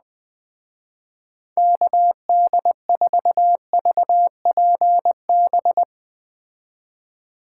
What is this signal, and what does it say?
Morse code, heard as a keyed tone, about 21 wpm, 700 Hz. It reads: KD4VPB
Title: Speed Racing - USA 2x3 Call Signs 17wpm